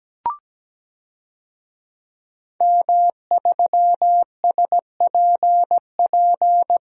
Morse code, heard as a keyed tone, about 17 wpm, 700 Hz. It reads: M3SPP
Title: DXCC-100 Call Signs - 5 Characters - 17wpm